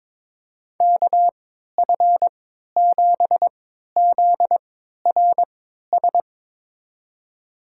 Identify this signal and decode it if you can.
Morse code, heard as a keyed tone, about 22 wpm, spaced as for 13 wpm, 700 Hz. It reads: KF7ZRS
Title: USA Call Signs - 6 Characters - 3x Character Spacing - 22wpm